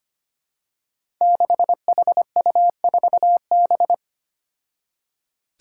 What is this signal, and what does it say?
Morse code, heard as a keyed tone, about 25 wpm, 700 Hz. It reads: 6HU4B
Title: Five Challenging Letter-Number Combinations 25wpm